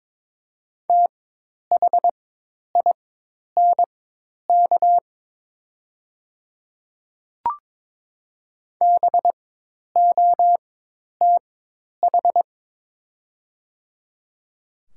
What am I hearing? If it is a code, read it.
Morse code, heard as a keyed tone, about 22 wpm, spaced as for 10 wpm, 700 Hz. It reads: THINK BOTH
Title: 4x Character Spacing - Top 300 Words - 22wpm